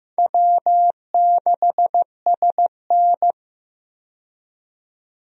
Morse code, heard as a keyed tone, about 15 wpm, 700 Hz. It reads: W6SN
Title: USA Call Signs - 4 Characters - 15wpm